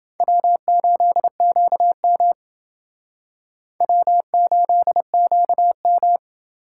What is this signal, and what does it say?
Morse code, heard as a keyed tone, about 28 wpm, 700 Hz. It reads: W8QM W8QM